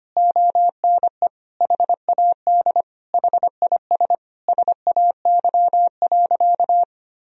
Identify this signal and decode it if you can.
Morse code, heard as a keyed tone, about 25 wpm, 700 Hz. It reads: ONE HAD HIS SAY.
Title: Sentences from Top 1000 Words - Review Part 1 - 25wpm